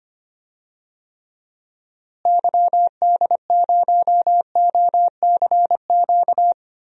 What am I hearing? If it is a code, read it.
Morse code, heard as a keyed tone, about 25 wpm, 700 Hz. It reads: YD0OCQ
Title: DXCC-100 Call Signs - 6 Characters - Repeated 3x - 25wpm